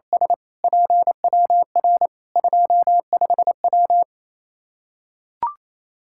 Morse code, heard as a keyed tone, about 28 wpm, 700 Hz. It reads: S PWR 25W
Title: Practice QSO - Exchange 2 - Info - 28wpm